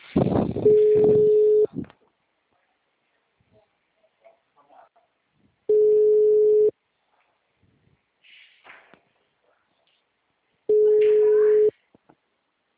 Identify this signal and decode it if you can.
Ringback tone (Europe, 425 Hz), with speech.